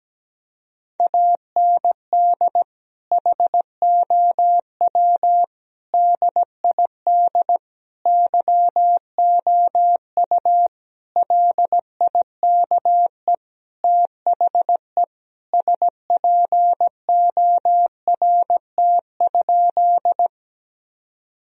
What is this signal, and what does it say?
Morse code, heard as a keyed tone, about 17 wpm, 700 Hz. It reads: AND HOW DID YOU LIKE THE SPORT?
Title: Sentences from Top 1000 Words - Review Part 4 - 17wpm